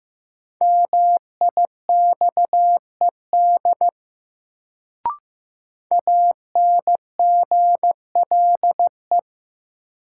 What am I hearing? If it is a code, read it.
Morse code, heard as a keyed tone, about 15 wpm, 700 Hz. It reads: MIXED ANGLE
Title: Sentences from Top 1500 Words - New Word Drill Alt - 15wpm